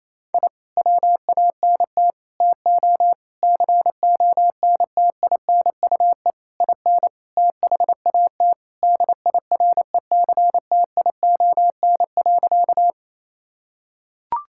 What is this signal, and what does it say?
Morse code, heard as a keyed tone, about 28 wpm, 690 Hz. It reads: I WANT TO CONTINUE IN THAT DIRECTION.
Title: Sentences from Top 1000 Words - Review Part 6 - 28wpm